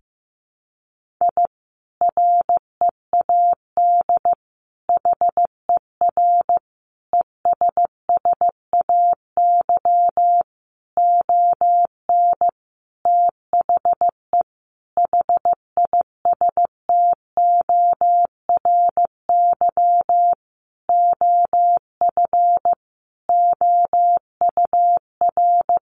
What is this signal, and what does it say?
Morse code, heard as a keyed tone, about 14 wpm, 700 Hz. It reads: I READ HER ESSAY ON THE HISTORY OF OUR
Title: Sentences from Top 1500 Words 15wpm